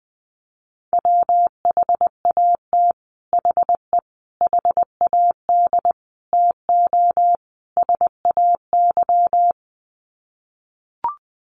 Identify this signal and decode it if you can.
Morse code, heard as a keyed tone, about 20 wpm, 700 Hz. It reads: WHAT HE HAD TO SAY
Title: Sets of 5 Words Encore2 20wpm